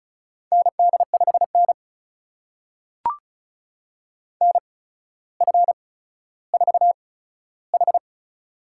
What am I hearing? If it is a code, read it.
Morse code, heard as a keyed tone, about 36 wpm, spaced as for 15 wpm, 700 Hz. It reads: ND5N NF4H
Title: USA Call Signs - 4 Characters - 8x Character Spacing - 35wpm